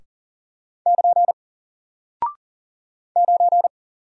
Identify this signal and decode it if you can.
Morse code, heard as a keyed tone, about 40 wpm, 700 Hz. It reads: <KN> 9
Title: Single Character (Letter:Number:Punctuation:Prosign) Alt 40wpm